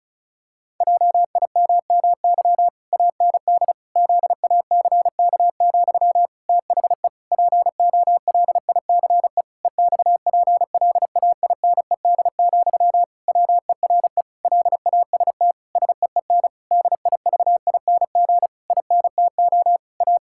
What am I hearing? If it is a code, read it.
Morse code, heard as a keyed tone, about 34 wpm, 700 Hz. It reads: JIMMY AND ZACK, THE POLICE EXPLAINED, WERE LAST SEEN DIVING INTO A
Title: Pangram Challenge v2 - 35wpm